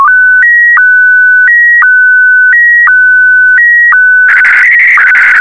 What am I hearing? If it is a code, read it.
SSTV (slow-scan television), Scottie DX.